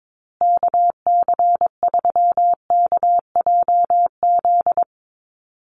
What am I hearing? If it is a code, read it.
Morse code, heard as a keyed tone, about 22 wpm, 700 Hz. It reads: KC3KJZ
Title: USA Call Signs Repeated 2x 22wpm